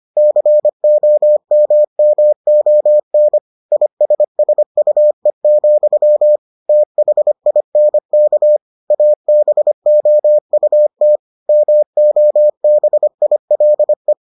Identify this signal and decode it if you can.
Morse code, heard as a keyed tone, about 25 wpm, 600 Hz. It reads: COMMON ISSUE, THINK ABOUT MOBILE